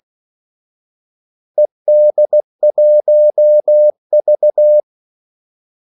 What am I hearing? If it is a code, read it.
Morse code, heard as a keyed tone, about 16 wpm, 600 Hz. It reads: ED1V
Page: Kaldesignaler 16 wpm